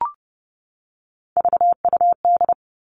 Morse code, heard as a keyed tone, about 30 wpm, 700 Hz. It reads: VUD
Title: Three Challenging Letter-Number Combinations 30wpm